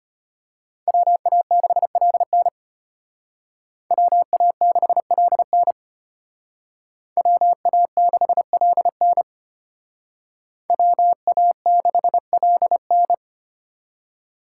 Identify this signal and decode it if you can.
Morse code, heard as a keyed tone, about 32 wpm, 700 Hz. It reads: WA6LN WA6LN WA6LN WA6LN